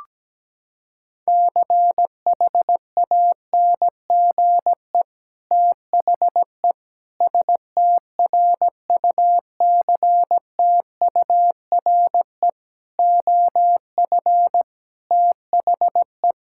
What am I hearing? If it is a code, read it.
Morse code, heard as a keyed tone, about 17 wpm, 700 Hz. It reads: CHANGE THE STRUCTURE OF THE
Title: Sentences from Top 1000 Words - Review Part 6 - 17wpm